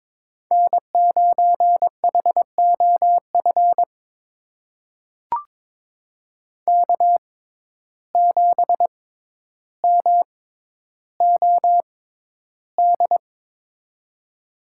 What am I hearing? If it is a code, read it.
Morse code, heard as a keyed tone, about 23 wpm, spaced as for 11 wpm, 700 Hz. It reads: N9HOF K7MOD
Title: USA Call Signs - 6x Character Spacing - 22wpm